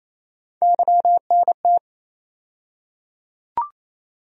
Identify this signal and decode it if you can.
Morse code, heard as a keyed tone, about 27 wpm, 700 Hz. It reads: YNT
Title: USA Call Sign Postfixes - 3 Letters 28wpm